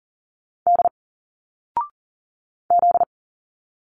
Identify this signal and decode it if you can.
Morse code, heard as a keyed tone, about 40 wpm, 700 Hz. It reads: D Z
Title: Single Letters 40wpm